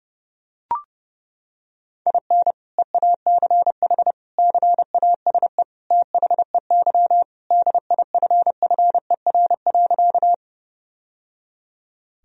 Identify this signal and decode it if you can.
Morse code, heard as a keyed tone, about 30 wpm, 700 Hz. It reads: IN EACH CASE THEY DIFFER.
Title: Sentences from Top 1000 Words - Review Part 1 - 30wpm